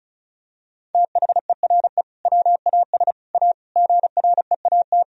Morse code, heard as a keyed tone, about 34 wpm, 700 Hz. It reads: THERE WAS A GREAT
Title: Sentences from Top 1000 Words - Review Part 4 - 35wpm